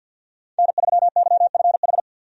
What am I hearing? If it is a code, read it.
Morse code, heard as a keyed tone, about 52 wpm, 700 Hz. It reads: N3YFH
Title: USA Call Signs - 5 Characters - 50wpm